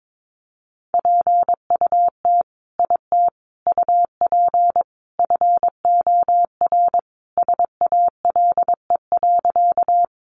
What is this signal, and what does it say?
Morse code, heard as a keyed tone, about 22 wpm, 700 Hz. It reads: PUT IT UP FOR SALE.
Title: Sentences from Top 1400 Words 22wpm